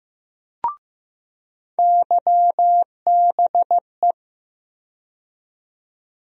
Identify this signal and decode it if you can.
Morse code, heard as a keyed tone, about 15 wpm, 700 Hz. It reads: YBE
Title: USA Call Sign Postfixes - 3 Letters 15wpm